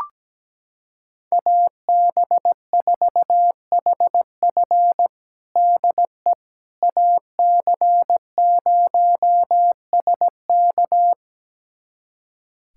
Morse code, heard as a keyed tone, about 17 wpm, 700 Hz. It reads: AB4HF DE AC0SK
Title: USA Callsign DE Callsign - 5 Characters - 17wpm